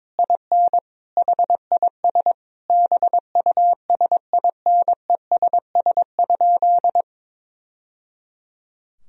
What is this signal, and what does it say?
Morse code, heard as a keyed tone, about 22 wpm, 700 Hz. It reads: IN HIS BUSINESS?
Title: Sentences from Top 1000 Words - Review Part 4 - 22wpm